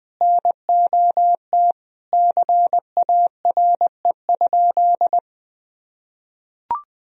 Morse code, heard as a keyed tone, about 20 wpm, 700 Hz. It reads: NOT CARE?
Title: Sentences from Top 1000 Words - Review Part 1 - 20wpm